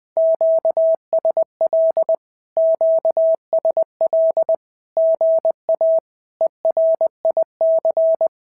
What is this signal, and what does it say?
Morse code, heard as a keyed tone, about 20 wpm, 650 Hz. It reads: QSL QSL GA ERIC
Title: SOTA QSO Practice 20wpm